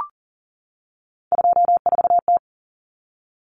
Morse code, heard as a keyed tone, about 40 wpm, 700 Hz. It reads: 24T
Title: Common Cut Numbers - 3-Digit 40wpm